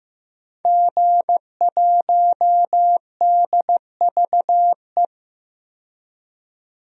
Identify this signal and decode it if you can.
Morse code, heard as a keyed tone, about 15 wpm, 700 Hz. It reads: G1DVE
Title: DXCC-100 Call Signs - 5 Characters - 15wpm